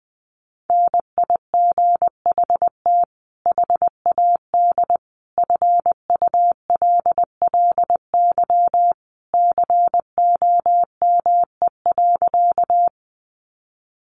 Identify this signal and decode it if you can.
Morse code, heard as a keyed tone, about 20 wpm, 700 Hz. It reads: NIGHT HAD FULLY COME.
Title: Sentences from Top 1300 Words 20wpm